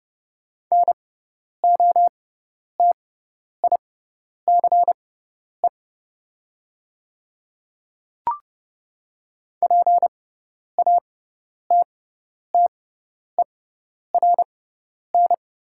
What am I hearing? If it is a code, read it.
Morse code, heard as a keyed tone, about 30 wpm, spaced as for 10 wpm, 700 Hz. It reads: NOTICE PATTERN
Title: 6x Character Spacing - Top 500 Words - 30wpm